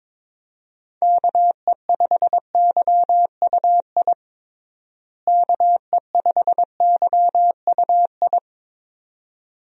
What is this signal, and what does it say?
Morse code, heard as a keyed tone, about 22 wpm, 700 Hz. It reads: KE5YUI KE5YUI
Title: USA Call Signs - 6 Characters - Repeated 2x - 22wpm